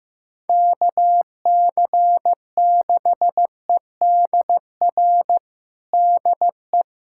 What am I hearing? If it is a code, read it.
Morse code, heard as a keyed tone, about 16 wpm, 700 Hz. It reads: KC6EDR DE
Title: USA Callsign DE Callsign 15wpm